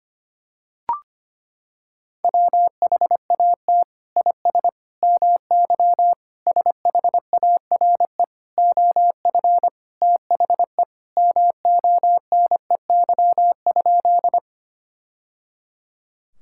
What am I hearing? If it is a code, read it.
Morse code, heard as a keyed tone, about 25 wpm, 700 Hz. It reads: WHAT IS MY SHARE OF THE MONEY?
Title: Sentences from Top 1000 Words - Review Part 6 - 25wpm